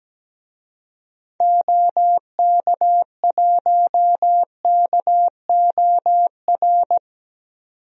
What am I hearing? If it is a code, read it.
Morse code, heard as a keyed tone, about 17 wpm, 700 Hz. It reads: OK1KOR
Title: DXCC-100 Call Signs - 6 Characters - 17wpm